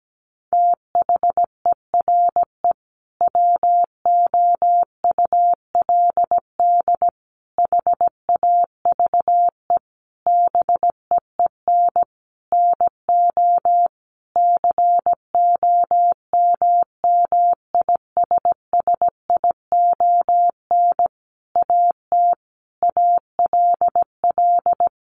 Morse code, heard as a keyed tone, about 17 wpm, 700 Hz. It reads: THERE WOULD HAVE BEEN NO COMMISSION AT ALL
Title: Sentences from Top 1400 Words 17wpm